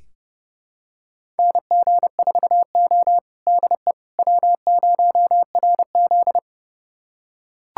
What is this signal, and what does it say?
Morse code, heard as a keyed tone, about 30 wpm, 700 Hz. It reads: NG4O DE W0RZ
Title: USA Callsign DE Callsign - 4 Characters - 30wpm